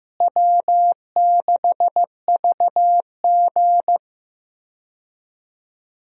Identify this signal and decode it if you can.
Morse code, heard as a keyed tone, about 15 wpm, 700 Hz. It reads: W6VG
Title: USA Call Signs - 4 Characters - Repeated 2x - 15wpm